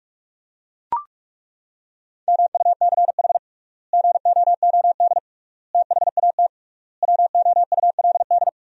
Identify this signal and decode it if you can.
Morse code, heard as a keyed tone, about 42 wpm, 700 Hz. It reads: MUCH GOOD THAT WOULD
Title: 3x Word-Spacing - Sentences from Top 200 Words - 45wpm